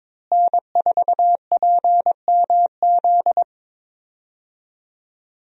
Morse code, heard as a keyed tone, about 22 wpm, 700 Hz. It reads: N4PMZ
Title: USA Call Signs - 5 Characters - Repeated 3x - 22wpm